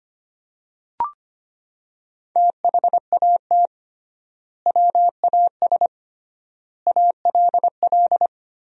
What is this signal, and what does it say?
Morse code, heard as a keyed tone, about 25 wpm, 700 Hz. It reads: THAT WAS ALL
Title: 3x Word-Spacing - Sentences from Top 100 Words - 25wpm